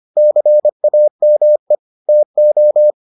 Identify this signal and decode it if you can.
Morse code, heard as a keyed tone, about 25 wpm, 600 Hz. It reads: CAME TO